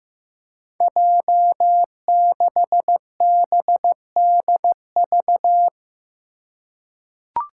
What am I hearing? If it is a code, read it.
Morse code, heard as a keyed tone, about 15 wpm, 700 Hz. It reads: J6BDV
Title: Five Challenging Letter-Number Combinations 15wpm